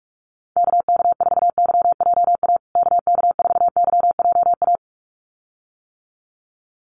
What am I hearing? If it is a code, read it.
Morse code, heard as a keyed tone, about 44 wpm, 700 Hz. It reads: KK4YJA KK4YJA
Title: USA Call Signs Repeated 2x 45wpm